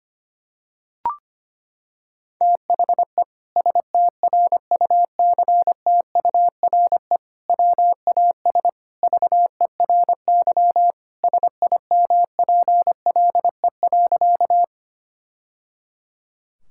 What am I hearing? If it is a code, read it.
Morse code, heard as a keyed tone, about 25 wpm, 700 Hz. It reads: THE STRUCTURE WAS VERY SIMPLE.
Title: Sentences from Top 1000 Words - Review Part 6 - 25wpm